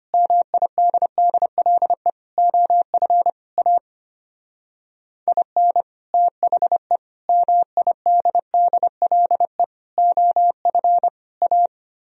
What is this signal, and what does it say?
Morse code, heard as a keyed tone, about 30 wpm, 700 Hz. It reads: MIDDLE OF A IN THE MIDDLE OF A
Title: Speed-Racing - Sets of 5 Words 25wpm